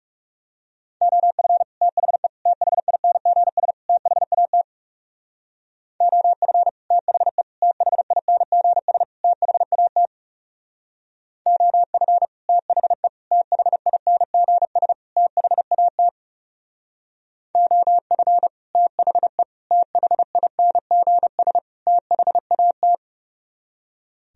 Morse code, heard as a keyed tone, about 36 wpm, 700 Hz. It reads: OF THE THINGS THAT OF THE THINGS THAT OF THE THINGS THAT OF THE THINGS THAT
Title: Speed-Racing - Sets of 4 Words 30wpm